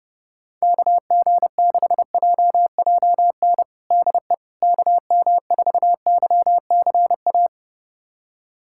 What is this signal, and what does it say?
Morse code, heard as a keyed tone, about 30 wpm, 700 Hz. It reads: KG6JJN DE KM4YCA
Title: USA Callsign DE Callsign - 6 Characters - 30wpm